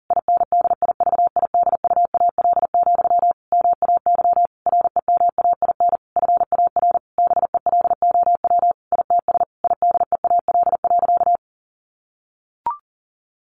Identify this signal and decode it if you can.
Morse code, heard as a keyed tone, about 40 wpm, 700 Hz. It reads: INDIVIDUAL, MAY REMAIN FAR BELOW ITS IDEAL.
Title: Sentences from Top 1400 Words 40wpm